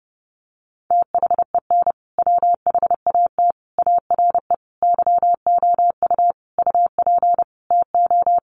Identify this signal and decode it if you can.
Morse code, heard as a keyed tone, about 30 wpm, 700 Hz. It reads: THEN WHAT ARE YOU UP TO
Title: Sentences from Top 100 Words 30wpm